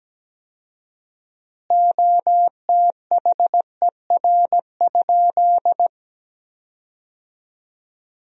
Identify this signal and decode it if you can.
Morse code, heard as a keyed tone, about 17 wpm, 700 Hz. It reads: OTHER?
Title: 4x Word-Spacing - Sentences from Top 200 Words - 17wpm